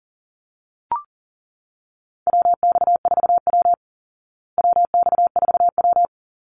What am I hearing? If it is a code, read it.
Morse code, heard as a keyed tone, about 40 wpm, 700 Hz. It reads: WX4W WX4W
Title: Call Signs Repeated 2x - 40wpm